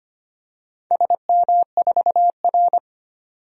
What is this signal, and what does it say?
Morse code, heard as a keyed tone, about 25 wpm, 700 Hz. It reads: SM4R
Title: DXCC-100 Call Signs - 4 Characters - 25wpm